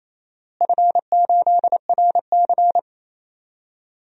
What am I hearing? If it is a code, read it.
Morse code, heard as a keyed tone, about 28 wpm, 700 Hz. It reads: F8RC
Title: DXCC-100 Call Signs - 4 Characters - Repeated 2x - 28wpm